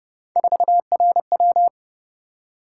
Morse code, heard as a keyed tone, about 28 wpm, 700 Hz. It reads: 4RW